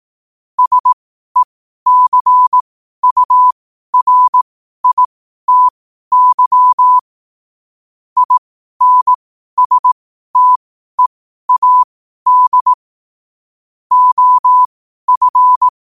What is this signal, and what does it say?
Morse code, heard as a keyed tone, about 18 wpm, spaced as for 12 wpm, 1000 Hz. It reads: SECURITY INSTEAD OF
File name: qotdfiles / 12wpm